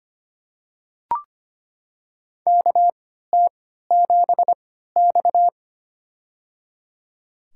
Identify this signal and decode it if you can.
Morse code, heard as a keyed tone, about 25 wpm, spaced as for 14 wpm, 700 Hz. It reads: KT7X
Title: USA Call Signs - 4 Characters - 3x Character Spacing - 25wpm